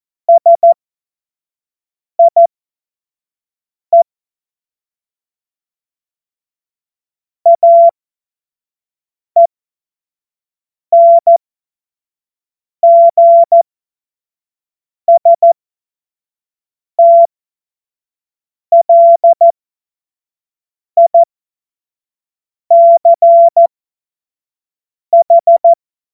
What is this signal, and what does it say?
Morse code, heard as a keyed tone, about 14 wpm, spaced as for 5 wpm, 680 Hz. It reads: SIE AENGSTLICH
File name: Morsekurs / Witze / Witz 28. März 2023 25 F70